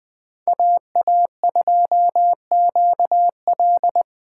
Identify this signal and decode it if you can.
Morse code, heard as a keyed tone, about 20 wpm, 700 Hz. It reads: AA2QL
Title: USA Call Signs - 5 Characters - 8x Character Spacing - 20wpm